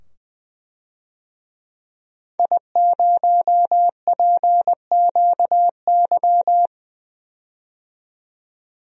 Morse code, heard as a keyed tone, about 20 wpm, 700 Hz. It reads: I0PQY
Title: DXCC-100 Call Signs - 5 Characters - 20wpm